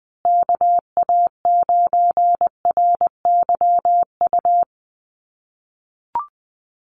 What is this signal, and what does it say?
Morse code, heard as a keyed tone, about 20 wpm, 700 Hz. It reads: KA9RYU
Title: USA Call Signs Repeated 2x 20wpm